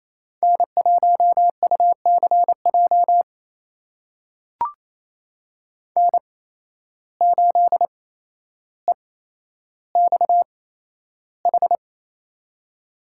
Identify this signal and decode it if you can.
Morse code, heard as a keyed tone, about 28 wpm, spaced as for 12 wpm, 700 Hz. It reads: N1UCJ N8EXH
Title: USA Call Signs - 5 Characters - 8x Character Spacing - 28wpm